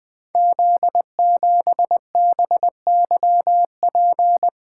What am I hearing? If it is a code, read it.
Morse code, heard as a keyed tone, about 20 wpm, 700 Hz. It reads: Z7BYP
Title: Code Groups - 5 Letter:Numbers - 20wpm